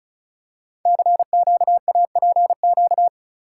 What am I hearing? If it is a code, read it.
Morse code, heard as a keyed tone, about 36 wpm, 700 Hz. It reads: CQAPQ
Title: Code Groups - 5 Letters - 35wpm